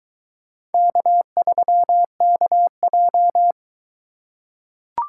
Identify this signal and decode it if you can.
Morse code, heard as a keyed tone, about 20 wpm, 700 Hz. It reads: K3KJ